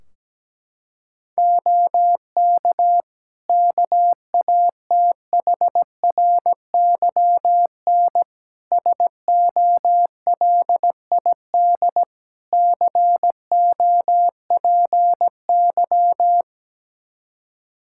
Morse code, heard as a keyed tone, about 17 wpm, 700 Hz. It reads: OK KATHRYN SOLID COPY
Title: K1USN SST QSO Practice 17wpm